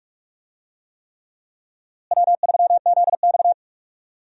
Morse code, heard as a keyed tone, about 43 wpm, 700 Hz. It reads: W3ZX
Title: USA Call Signs - 4 Characters - Repeated 3x - 45wpm